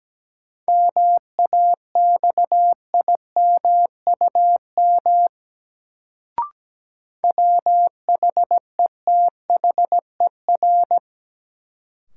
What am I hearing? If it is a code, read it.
Morse code, heard as a keyed tone, about 17 wpm, 700 Hz. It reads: MAXIMUM WHETHER
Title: Sentences from Top 1300 Words - New Word Drill Alt - 17wpm